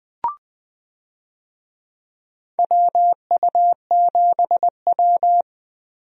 Morse code, heard as a keyed tone, about 20 wpm, 700 Hz. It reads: WU7W
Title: USA Call Signs - 4 Characters - Repeated 3x - 20wpm